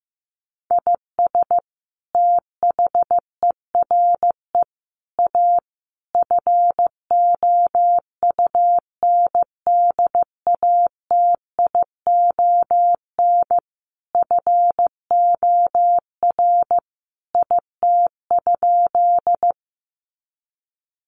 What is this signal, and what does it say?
Morse code, heard as a keyed tone, about 15 wpm, 700 Hz. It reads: IS THERE A FOUNDATION FOR IT?
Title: Sentences from Top 1200 Words 15wpm